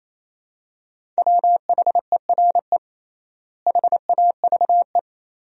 Morse code, heard as a keyed tone, about 28 wpm, 700 Hz. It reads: WHERE HAVE
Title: 3x Word-Spacing - Sentences from Top 200 Words - 28wpm